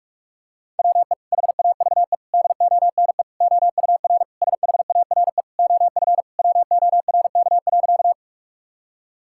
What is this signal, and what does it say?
Morse code, heard as a keyed tone, about 46 wpm, 700 Hz. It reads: WE HAVE DONE OUR SHARE OF WORK.
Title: Sentences from Top 1000 Words - Review Part 6 - 45wpm